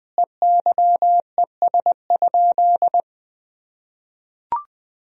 Morse code, heard as a keyed tone, about 21 wpm, 700 Hz. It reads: EYES?
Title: Sentences from Top 1000 Words - Review Part 4 - 20wpm